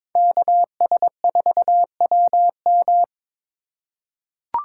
Morse code, heard as a keyed tone, about 22 wpm, 700 Hz. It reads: KS4WM
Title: USA Call Signs - 5 Characters - 4x Character Spacing - 22wpm